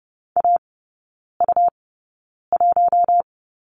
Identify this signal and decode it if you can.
Morse code, heard as a keyed tone, about 30 wpm, spaced as for 9 wpm, 690 Hz. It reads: AU1
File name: Lesson 17e - Letter u - Three Characters 30wpm